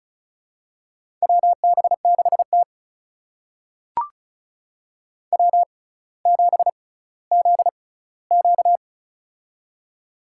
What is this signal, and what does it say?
Morse code, heard as a keyed tone, about 36 wpm, spaced as for 18 wpm, 700 Hz. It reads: WB6T W7ZQ
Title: USA Call Signs - 4 Characters - 6x Character Spacing - 35wpm